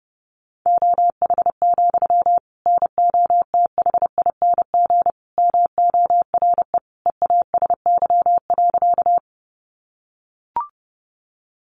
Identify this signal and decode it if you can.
Morse code, heard as a keyed tone, about 30 wpm, 700 Hz. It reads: OH, NOTHING MORE EASY.
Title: Sentences from Top 700 Words 30wpm